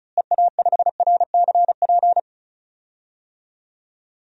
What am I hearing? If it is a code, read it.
Morse code, heard as a keyed tone, about 36 wpm, 700 Hz. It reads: EA5RCP
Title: DXCC-100 Call Signs - 6 Characters - Repeated 3x - 35wpm